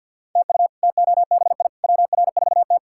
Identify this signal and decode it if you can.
Morse code, heard as a keyed tone, about 46 wpm, 700 Hz. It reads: TU TOBI WR4T
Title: K1USN SST QSO Practice - 50wpm